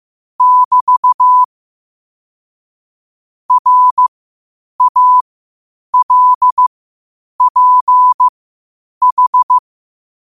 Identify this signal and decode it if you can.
Morse code, heard as a keyed tone, about 15 wpm, spaced as for 8 wpm, 1000 Hz. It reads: <BT> RALPH
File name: qotdfiles / 8wpm